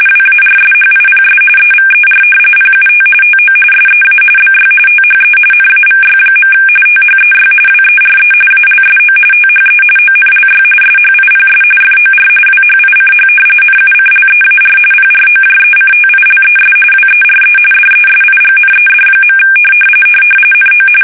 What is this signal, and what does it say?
RTTY, about 45.45 baud. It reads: WARSHIPS EVER PUT TO SEA BY THE NAVY. THEY WERE ALSO THE ONLY SHIPS FOR WHICH NUCLEAR PROJECTILES WERE DESIGNED. NO
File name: fsk850 / battleshipnj